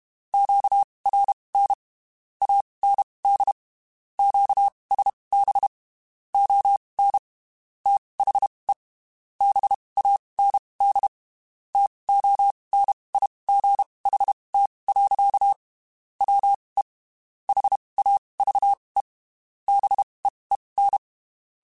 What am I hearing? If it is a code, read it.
Morse code, heard as a keyed tone, about 32 wpm, spaced as for 23 wpm, 800 Hz. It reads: QRN AND QSB ON THE BAND TONIGHT. WE HAVE BEEN
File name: (spaced) / QSOs Spaced-mp3s / WD0BGP 1 @30 sp